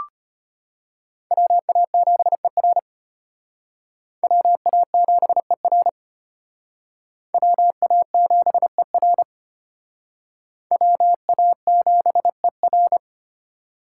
Morse code, heard as a keyed tone, about 31 wpm, 700 Hz. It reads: WA7ER WA7ER WA7ER WA7ER